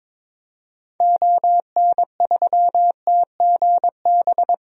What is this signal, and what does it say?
Morse code, heard as a keyed tone, about 21 wpm, 700 Hz. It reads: ON3TGB
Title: DXCC-100 Call Signs - 6 Characters - Repeated 2x - 22wpm